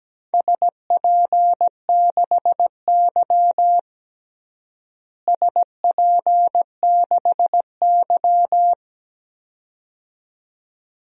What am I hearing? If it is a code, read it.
Morse code, heard as a keyed tone, about 17 wpm, 700 Hz. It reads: SP6Y SP6Y
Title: DXCC-100 Call Signs - 4 Characters - Repeated 2x - 17wpm